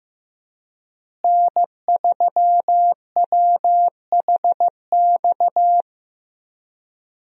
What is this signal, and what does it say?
Morse code, heard as a keyed tone, about 16 wpm, 700 Hz. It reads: N3WHX
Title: USA Call Signs - 5 Characters - Repeated 3x - 15wpm